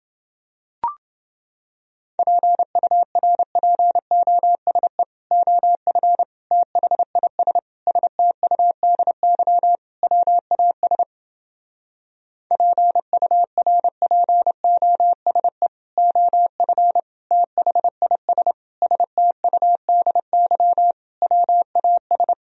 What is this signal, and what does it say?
Morse code, heard as a keyed tone, about 31 wpm, 700 Hz. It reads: PURPOSE OF THIS STUDY WAS PURPOSE OF THIS STUDY WAS